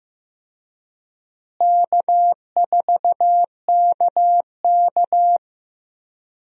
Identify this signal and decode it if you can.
Morse code, heard as a keyed tone, about 16 wpm, 700 Hz. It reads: K4KK
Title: USA Call Signs - 4 Characters - Repeated 3x - 15wpm